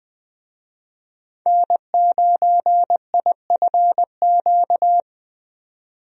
Morse code, heard as a keyed tone, about 20 wpm, 700 Hz. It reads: N9IFQ
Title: USA Call Signs - 5 Characters - 20wpm